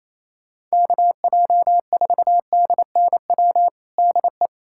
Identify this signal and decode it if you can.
Morse code, heard as a keyed tone, about 27 wpm, 700 Hz. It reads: KJ4DNW DE
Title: USA Callsign DE Callsign - 6 Characters - 28wpm